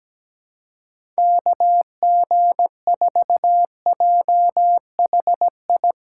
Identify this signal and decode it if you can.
Morse code, heard as a keyed tone, about 17 wpm, 700 Hz. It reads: KG4JHI
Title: USA Callsign DE Callsign 17wpm